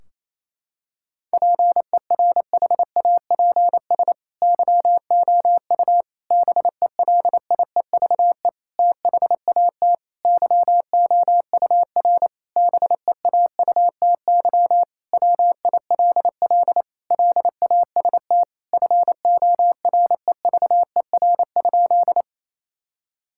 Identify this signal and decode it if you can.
Morse code, heard as a keyed tone, about 28 wpm, 700 Hz. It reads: PERHAPS YOU BELIEVE THAT YOUR BEAUTY WILL LAST FOREVER?
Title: Sentences from Top 1600 Words 28wpm